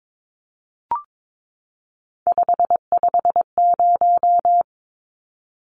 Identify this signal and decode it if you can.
Morse code, heard as a keyed tone, about 22 wpm, 700 Hz. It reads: 550
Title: Numbers - 3-Digit 22wpm